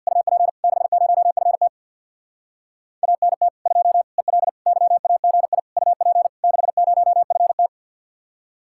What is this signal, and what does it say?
Morse code, heard as a keyed tone, about 60 wpm, 700 Hz. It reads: UP 60FT ANT 2 EL YAGI UP 60FT
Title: Speed-Racing Common QSO Phrases Encore - 50wpm